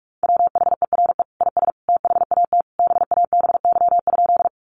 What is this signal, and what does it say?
Morse code, heard as a keyed tone, about 44 wpm, 700 Hz. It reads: WHERE IS THAT BABY?
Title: Sentences from Top 1300 Words 45wpm